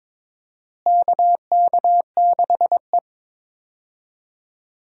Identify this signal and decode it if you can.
Morse code, heard as a keyed tone, about 21 wpm, 700 Hz. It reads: KK6E
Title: USA Call Signs - 4 Characters - Repeated 3x - 22wpm